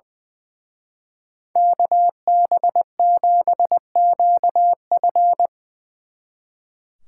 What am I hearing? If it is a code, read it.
Morse code, heard as a keyed tone, about 23 wpm, 700 Hz. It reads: KB7QF
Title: Speed Racing - USA Call Signs 20wpm